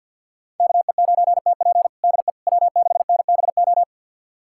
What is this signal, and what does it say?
Morse code, heard as a keyed tone, about 50 wpm, 700 Hz. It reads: KE9TP DE W6NBO
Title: USA Callsign DE Callsign - 5 Characters - 50wpm